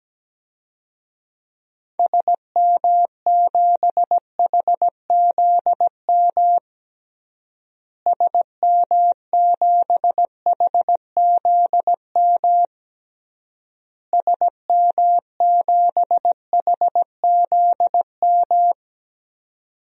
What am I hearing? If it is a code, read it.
Morse code, heard as a keyed tone, about 17 wpm, 700 Hz. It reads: SM7HZM SM7HZM SM7HZM
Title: DXCC-100 Call Signs - 6 Characters - Repeated 3x - 17wpm